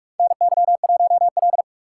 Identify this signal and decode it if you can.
Morse code, heard as a keyed tone, about 42 wpm, 700 Hz. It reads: NY1L
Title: USA Call Signs - 4 Characters - 3x Character Spacing - 45wpm